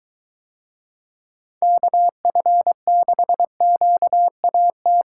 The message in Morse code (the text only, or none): KF6QAT